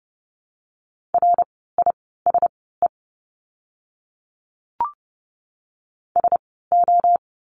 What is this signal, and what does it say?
Morse code, heard as a keyed tone, about 30 wpm, spaced as for 17 wpm, 700 Hz. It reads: RISE SO
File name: Lesson 16f - Letter c - Words 30wpm